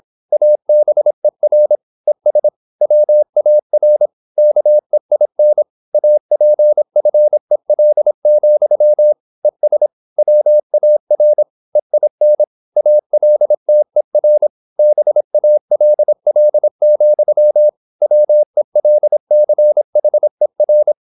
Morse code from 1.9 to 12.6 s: ES WAR KEIN APFEL, ES WAR EIN